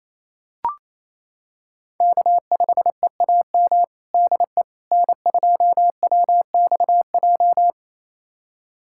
Morse code, heard as a keyed tone, about 28 wpm, 690 Hz. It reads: K5EAM DE N2WXJ
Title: USA Callsign DE Callsign - 5 Characters - 28wpm